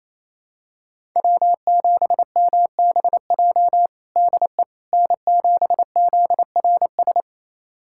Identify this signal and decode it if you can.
Morse code, heard as a keyed tone, about 28 wpm, 700 Hz. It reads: W7MBJ DE N7ZRS
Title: USA Callsign DE Callsign - 5 Characters - 28wpm